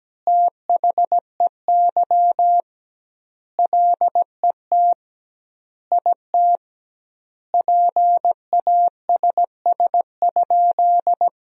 Morse code, heard as a keyed tone, about 17 wpm, 700 Hz. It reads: THEY LET IT PASS?
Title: 2x Word-Spacing - Sentences from Top 400 Words - 17wpm